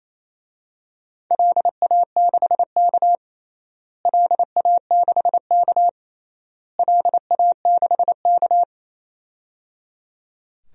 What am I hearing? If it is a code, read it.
Morse code, heard as a keyed tone, about 28 wpm, 700 Hz. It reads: LA6K LA6K LA6K
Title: DXCC-100 Call Signs - 4 Characters - Repeated 3x - 28wpm